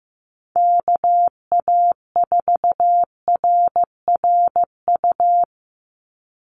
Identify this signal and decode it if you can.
Morse code, heard as a keyed tone, about 15 wpm, 700 Hz. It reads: KA4RRU
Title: Call Signs Repeated 3x - 15wpm